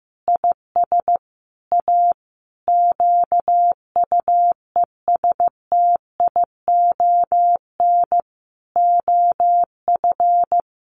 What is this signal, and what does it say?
Morse code, heard as a keyed tone, about 15 wpm, 700 Hz. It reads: IS A QUESTION OF
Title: Sentences from Top 1200 Words 15wpm